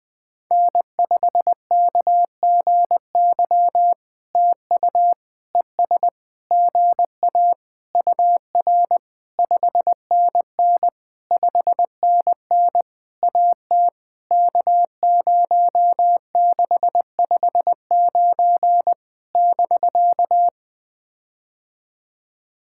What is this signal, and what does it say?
Morse code, heard as a keyed tone, about 20 wpm, 700 Hz. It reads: N5KGY TU ES GA UR 5NN 5NN AT K0659 <BK>
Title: POTA QSO Practice - 20wpm